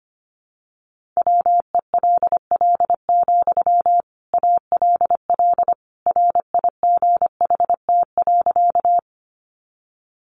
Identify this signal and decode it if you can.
Morse code, heard as a keyed tone, about 25 wpm, 700 Hz. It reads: WELL, ALL RIGHT.
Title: Sentences from Top 200 Words 25wpm